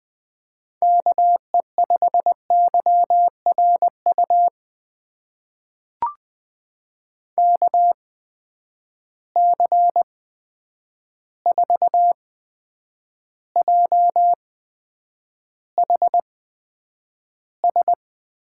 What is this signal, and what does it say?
Morse code, heard as a keyed tone, about 20 wpm, spaced as for 9 wpm, 700 Hz. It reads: KE5YRU KC4JHS
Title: USA Call Signs - 6 Characters - 8x Character Spacing - 20wpm